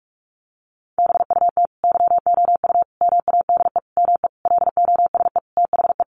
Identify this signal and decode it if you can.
Morse code, heard as a keyed tone, about 44 wpm, 700 Hz. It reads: BUT YOU MADE ME LOSE THE
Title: Sentences from Top 1300 Words 45wpm